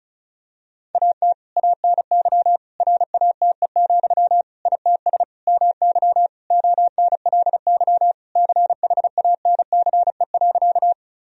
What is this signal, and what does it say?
Morse code, heard as a keyed tone, about 36 wpm, 700 Hz. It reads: AT ANY RATE, ITS MY ONLY CHANCE.
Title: Sentences from Top 1000 Words - Review Part 4 - 35wpm